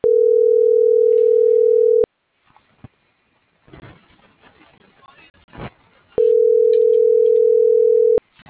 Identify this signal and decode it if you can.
Ringback tone (North America), 440 + 480 Hz.